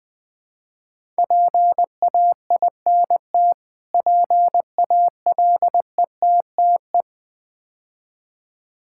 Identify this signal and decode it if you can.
Morse code, heard as a keyed tone, about 20 wpm, 700 Hz. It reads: PAINT PALETTE
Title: Difficult Words - Speed-Racing with Hints - 20wpm